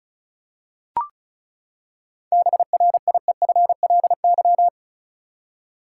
Morse code, heard as a keyed tone, about 34 wpm, 700 Hz. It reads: BRIEFLY
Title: Seven Letter Words - 35wpm